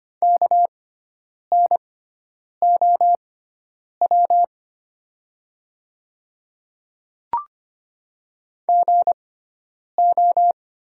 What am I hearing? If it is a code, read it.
Morse code, heard as a keyed tone, about 25 wpm, spaced as for 9 wpm, 700 Hz. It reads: KNOW GO
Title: 6x Character Spacing - Top 100 Words - 25wpm